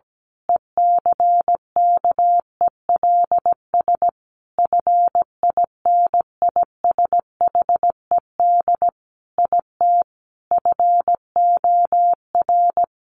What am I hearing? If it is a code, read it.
Morse code, heard as a keyed tone, about 17 wpm, 700 Hz. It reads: ECKELS FINISHED IT FOR
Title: A Sound Of Thunder 17wpm